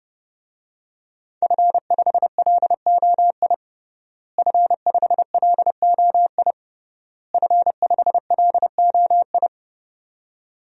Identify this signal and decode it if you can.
Morse code, heard as a keyed tone, about 32 wpm, 700 Hz. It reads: F5LOI F5LOI F5LOI
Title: DXCC-100 Call Signs - 5 Characters - Repeated 3x - 30wpm